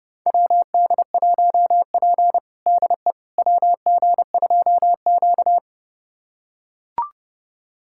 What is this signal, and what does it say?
Morse code, heard as a keyed tone, about 30 wpm, 700 Hz. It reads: WD1P DE WG2Q
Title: USA Callsign DE Callsign - 4 Characters - 30wpm